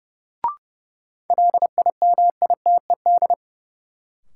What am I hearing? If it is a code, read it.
Morse code, heard as a keyed tone, about 32 wpm, 700 Hz. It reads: LIMITED
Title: Sentences from Top 900 Words - New Word Drill - 30wpm